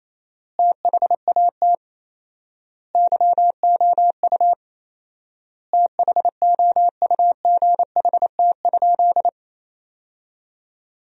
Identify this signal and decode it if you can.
Morse code, heard as a keyed tone, about 28 wpm, 700 Hz. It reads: THAT YOU THOUGHT?
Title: 4x Word-Spacing - Sentences from Top 300 Words - 28wpm